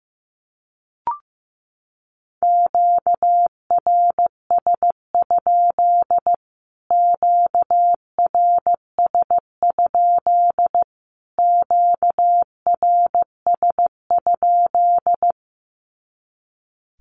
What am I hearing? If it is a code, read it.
Morse code, heard as a keyed tone, about 15 wpm, 700 Hz. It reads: QRS? QRS? QRS?
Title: Q-Codes Repeated 3x 15wpm